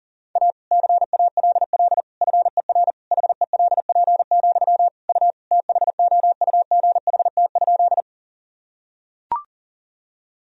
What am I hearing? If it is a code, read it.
Morse code, heard as a keyed tone, about 40 wpm, 700 Hz. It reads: A CALL FER HELP, U THOUGHT?
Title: Sentences from Top 300 Words and Abbreviations 40wpm